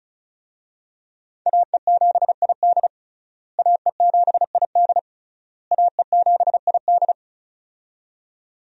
Morse code, heard as a keyed tone, about 36 wpm, 700 Hz. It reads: AE7ID AE7ID AE7ID
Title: USA Call Signs - 5 Characters - Repeated 3x - 35wpm